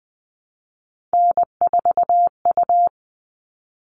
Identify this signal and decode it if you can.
Morse code, heard as a keyed tone, about 20 wpm, 700 Hz. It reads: N4U
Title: All Cut Numbers - 3-Digit 20wpm